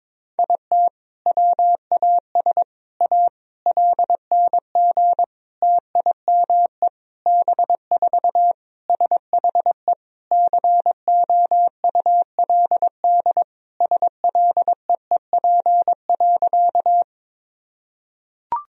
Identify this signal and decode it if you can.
Morse code, heard as a keyed tone, about 22 wpm, 700 Hz. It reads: IT WAS A LNG TIME B4 SHE COULD SLEEP.
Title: Sentences from Top 500 Words and Abbreviations 22wpm